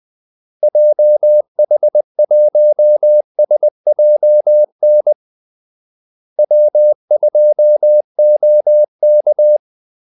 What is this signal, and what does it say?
Morse code, heard as a keyed tone, about 20 wpm, 600 Hz. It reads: JH1SJN W2OK
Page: Kaldesignaler 20 wpm